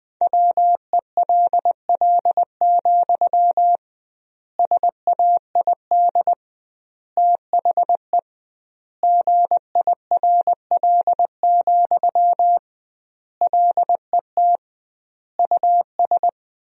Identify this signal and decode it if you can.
Morse code, heard as a keyed tone, about 20 wpm, 700 Hz. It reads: WELL, SAID THE GIRL, LET US
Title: 2x Word-Spacing - Sentences from Top 400 Words - 20wpm